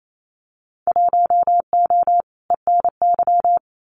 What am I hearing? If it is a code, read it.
Morse code, heard as a keyed tone, about 28 wpm, 700 Hz. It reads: 1O ENY
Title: Winter Field Day Exchanges 28wpm